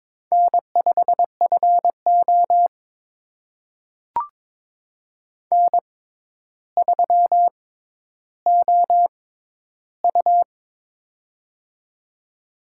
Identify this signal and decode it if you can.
Morse code, heard as a keyed tone, about 22 wpm, spaced as for 11 wpm, 700 Hz. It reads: N5FO N3OU
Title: USA Call Signs - 4 Characters - 6x Character Spacing - 22wpm